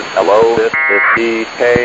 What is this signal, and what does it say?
Packet radio (AX.25, as in APRS), with speech.